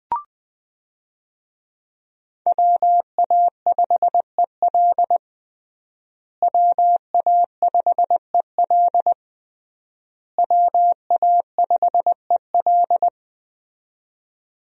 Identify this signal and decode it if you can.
Morse code, heard as a keyed tone, about 21 wpm, 700 Hz. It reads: WA5EL WA5EL WA5EL
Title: USA Call Signs - 5 Characters - Repeated 3x - 20wpm